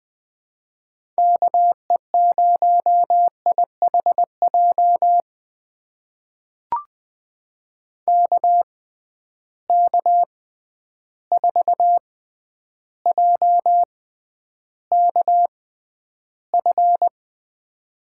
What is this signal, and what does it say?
Morse code, heard as a keyed tone, about 20 wpm, spaced as for 10 wpm, 700 Hz. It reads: KE0IHJ KK4JKF
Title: USA Call Signs - 6 Characters - 6x Character Spacing - 20wpm